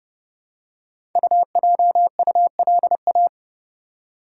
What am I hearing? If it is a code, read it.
Morse code, heard as a keyed tone, about 30 wpm, 700 Hz. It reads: UJULA
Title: Code Groups - 5 Letters - 30wpm